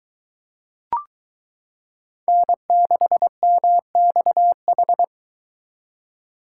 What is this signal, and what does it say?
Morse code, heard as a keyed tone, about 23 wpm, 700 Hz. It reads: N6MXH